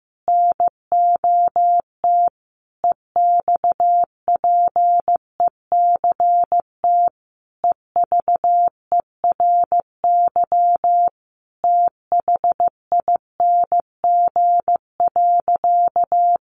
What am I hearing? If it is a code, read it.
Morse code, heard as a keyed tone, about 15 wpm, 700 Hz. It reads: NOT EXPECT EVERY THING.
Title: Sentences from Top 1100 Words 15wpm